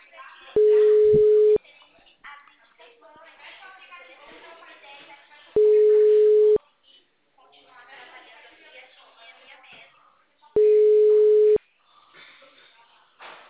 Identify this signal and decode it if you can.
Ringback tone (Europe, 425 Hz), with speech.